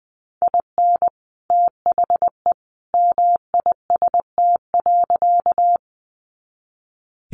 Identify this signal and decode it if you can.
Morse code, heard as a keyed tone, about 20 wpm, 700 Hz. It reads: IN THE MIST.
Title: A Sound Of Thunder 20wpm